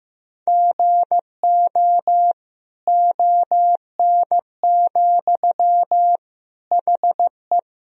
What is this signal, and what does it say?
Morse code, heard as a keyed tone, about 15 wpm, 700 Hz. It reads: GO ON, HE
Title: Sentences from Top 100 Words and Abbreviations 15wpm